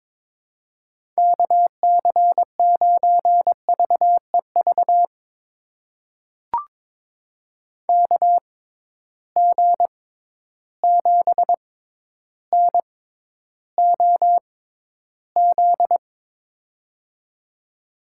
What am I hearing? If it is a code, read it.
Morse code, heard as a keyed tone, about 22 wpm, spaced as for 11 wpm, 700 Hz. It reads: KC9VEV KG7NOZ
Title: USA Call Signs - 6 Characters - 6x Character Spacing - 22wpm